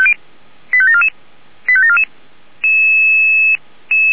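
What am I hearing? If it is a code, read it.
Five-tone selective call (ZVEI), heard as quick runs of five tones, tones of about 70 ms. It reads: 85744 85744
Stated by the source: ZVEI five-tone selective call 85748 received by SDR.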